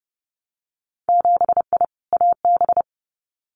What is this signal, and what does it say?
Morse code, heard as a keyed tone, about 30 wpm, 700 Hz. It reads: 7I AB
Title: Winter Field Day Exchanges Encore 30wpm